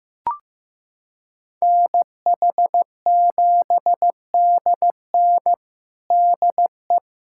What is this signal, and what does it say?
Morse code, heard as a keyed tone, about 15 wpm, 700 Hz. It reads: NH7DN DE
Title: USA Callsign DE Callsign 15wpm